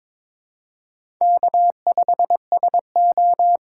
Morse code, heard as a keyed tone, about 26 wpm, 700 Hz. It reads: K5SO
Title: Speed Racing - USA 1x2 Call Signs 22wpm